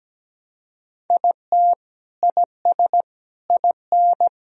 Morse code, heard as a keyed tone, about 17 wpm, 700 Hz. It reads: IT IS IN
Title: Sentences from Top 1000 Words - Review Part 1 - 17wpm